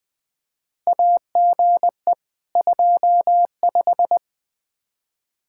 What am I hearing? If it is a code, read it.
Morse code, heard as a keyed tone, about 21 wpm, 700 Hz. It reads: AGE 25
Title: Practice QSO - Exchange 3 - More Info - 20wpm